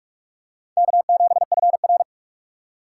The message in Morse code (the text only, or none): K7FR